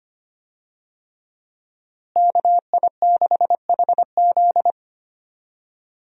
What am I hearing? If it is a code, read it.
Morse code, heard as a keyed tone, about 25 wpm, 700 Hz. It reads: KI6HZ
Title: USA Call Signs - 5 Characters - Repeated 3x - 25wpm